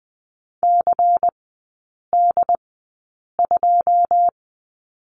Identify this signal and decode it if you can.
Morse code, heard as a keyed tone, about 20 wpm, spaced as for 8 wpm, 700 Hz. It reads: CD2
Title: 100 Lesson 17e - Letter u - Three Characters 20wpm